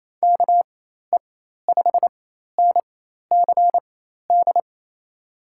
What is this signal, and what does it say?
Morse code, heard as a keyed tone, about 28 wpm, spaced as for 13 wpm, 700 Hz. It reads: KE5NCD
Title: USA Call Signs - 6 Characters - 4x Character Spacing - 28wpm